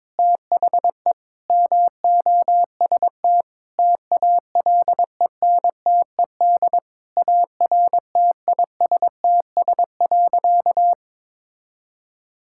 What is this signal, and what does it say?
Morse code, heard as a keyed tone, about 22 wpm, 700 Hz. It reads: THE MOST TALENTED ARTISTS.
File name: practice / 22wpm / Sentences with 8-Word Phrases - 22wpm